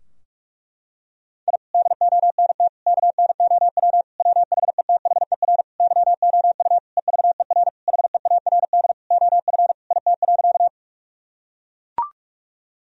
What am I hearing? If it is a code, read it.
Morse code, heard as a keyed tone, about 41 wpm, 700 Hz. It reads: I DONT KNOW WHETHER YOU EVER HEARD OF IT.
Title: Sentences from Top 1000 Words - Review Part 3 - 45wpm